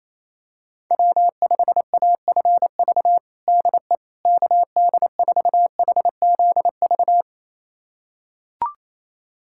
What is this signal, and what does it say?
Morse code, heard as a keyed tone, about 28 wpm, 700 Hz. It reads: W5AFV DE KD4HZV
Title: USA Callsign DE Callsign 28wpm